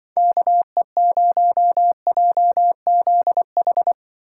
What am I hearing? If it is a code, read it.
Morse code, heard as a keyed tone, about 25 wpm, 700 Hz. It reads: KE0JZH
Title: Speed Racing - USA 2x3 Call Signs 20wpm